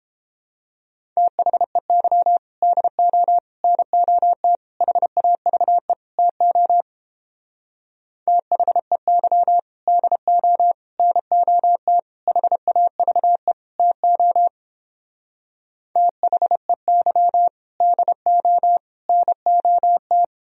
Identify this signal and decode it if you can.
Morse code, heard as a keyed tone, about 29 wpm, 700 Hz. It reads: THEY DO NOT HAVE TO THEY DO NOT HAVE TO THEY DO NOT